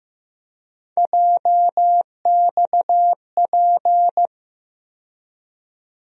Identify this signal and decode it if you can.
Morse code, heard as a keyed tone, about 15 wpm, 700 Hz. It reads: JXP
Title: USA Call Sign Postfixes - 3 Letters 15wpm